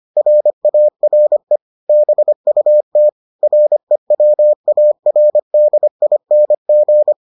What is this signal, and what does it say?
Morse code, heard as a keyed tone, about 25 wpm, 600 Hz. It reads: RARE BUT REWARDING